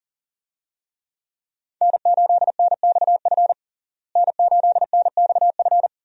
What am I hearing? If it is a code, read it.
Morse code, heard as a keyed tone, about 40 wpm, 700 Hz. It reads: N8NXF N8NXF
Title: USA Call Signs - 5 Characters - Repeated 2x - 40wpm